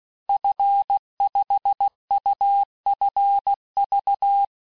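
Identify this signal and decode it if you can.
Morse code, heard as a keyed tone, about 16 wpm, 800 Hz. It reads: F5UFV
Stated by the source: Morse course text, end of QSO, sent at 15 words per minute.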